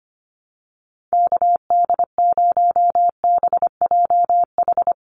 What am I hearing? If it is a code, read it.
Morse code, heard as a keyed tone, about 25 wpm, 700 Hz. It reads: KD0BJH
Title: USA Call Signs 25wpm